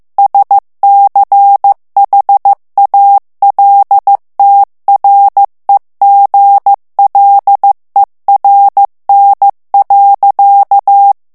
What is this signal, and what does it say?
Morse code, heard as a keyed tone, about 15 wpm, 800 Hz. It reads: SCHALTREGLERN.
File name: morse mp3 / ALLEMAND texte clair 1 vitesse 14 mots minute